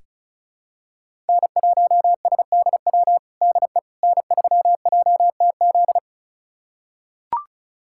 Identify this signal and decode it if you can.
Morse code, heard as a keyed tone, about 36 wpm, 700 Hz. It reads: N1SDW DE N3JTZ
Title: USA Callsign DE Callsign - 5 Characters - 35wpm